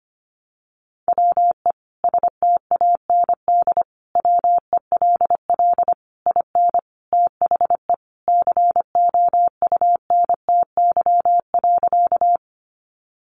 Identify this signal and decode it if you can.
Morse code, heard as a keyed tone, about 25 wpm, 700 Hz. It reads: WE STAND WELL IN THE COUNTY.
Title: Sentences from Top 1200 Words 25wpm